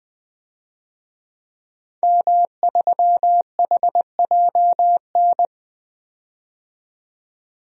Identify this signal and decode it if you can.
Morse code, heard as a keyed tone, about 20 wpm, 700 Hz. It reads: M3HJN
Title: DXCC-100 Call Signs - 5 Characters - 20wpm